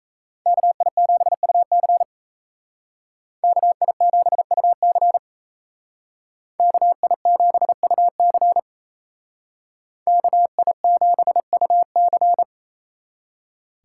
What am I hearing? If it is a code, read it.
Morse code, heard as a keyed tone, about 36 wpm, 700 Hz. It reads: KI7UC KI7UC KI7UC KI7UC